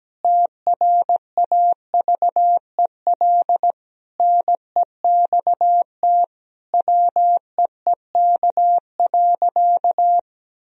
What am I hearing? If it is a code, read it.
Morse code, heard as a keyed tone, about 17 wpm, 700 Hz. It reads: TRAVEL NEXT WEEK.
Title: Sentences from Top 1000 Words - Review Part 2 - 17wpm